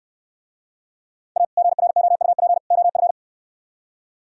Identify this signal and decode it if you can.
Morse code, heard as a keyed tone, about 66 wpm, 700 Hz. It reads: A GROUP OF